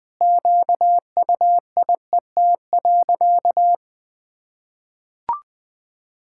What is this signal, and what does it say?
Morse code, heard as a keyed tone, about 20 wpm, 700 Hz. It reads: QUIET.
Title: Sentences with 7-Word Phrases - 20wpm